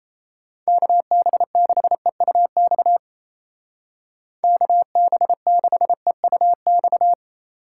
Morse code, heard as a keyed tone, about 35 wpm, 700 Hz. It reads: KB6EUX KB6EUX
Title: Speed Racing - USA 2x3 Call Signs 28wpm